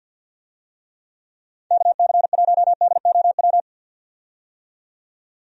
Morse code, heard as a keyed tone, about 51 wpm, 700 Hz. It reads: KC1DOW
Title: USA Call Signs - 6 Characters - 50wpm